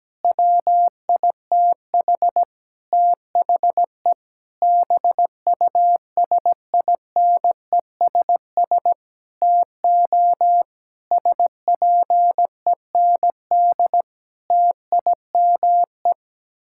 Morse code, heard as a keyed tone, about 17 wpm, 700 Hz. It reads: WITH THE BUSINESS TO SPEND TIME
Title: Sentences from Top 1000 Words - Review Part 6 - 17wpm